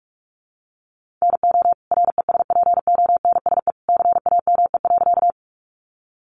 Morse code, heard as a keyed tone, about 44 wpm, 700 Hz. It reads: NO RESPONSE CAME.
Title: Sentences from Top 1400 Words 45wpm